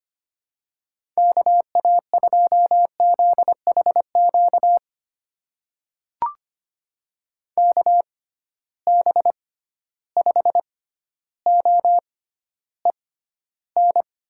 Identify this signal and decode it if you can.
Morse code, heard as a keyed tone, about 25 wpm, spaced as for 13 wpm, 700 Hz. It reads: KA2ZHQ KB5OEN
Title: USA Call Signs - 6 Characters - 6x Character Spacing - 25wpm